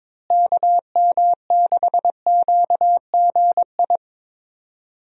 Morse code, heard as a keyed tone, about 22 wpm, 700 Hz. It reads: KM6QGI
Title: USA Call Signs - 6 Characters - Repeated 2x - 22wpm